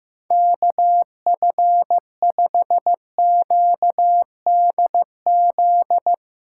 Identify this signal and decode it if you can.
Morse code, heard as a keyed tone, about 15 wpm, 700 Hz. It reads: KF5QDZ
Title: USA Call Signs - 6 Characters - 8x Character Spacing - 15wpm